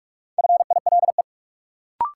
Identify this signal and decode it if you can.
Morse code, heard as a keyed tone, about 43 wpm, 700 Hz. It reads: FILE
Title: Sentences from Top 1200 Words - New Word Drill Alt - 45wpm